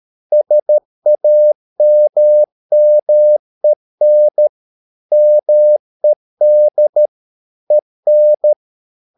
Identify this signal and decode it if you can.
Morse code, heard as a keyed tone, about 13 wpm, 600 Hz. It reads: SAMMEN MED EN
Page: Ferie 13 wpm